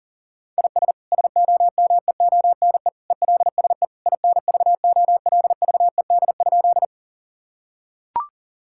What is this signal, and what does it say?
Morse code, heard as a keyed tone, about 40 wpm, 700 Hz. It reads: IS SOMEONE ELSE INVOLVED?
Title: Sentences from Top 1000 Words - Review Part 5 - 40wpm